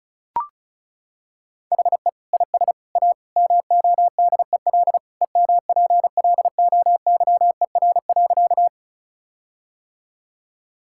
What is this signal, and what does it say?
Morse code, heard as a keyed tone, about 36 wpm, 700 Hz. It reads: HE IS A MODEL EMPLOYER.
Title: Sentences from Top 1600 Words 35wpm